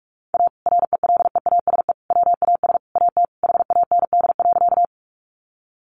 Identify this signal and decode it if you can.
Morse code, heard as a keyed tone, about 44 wpm, 700 Hz. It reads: A RELEASE WAS AT HAND.
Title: Sentences from Top 1100 Words 45wpm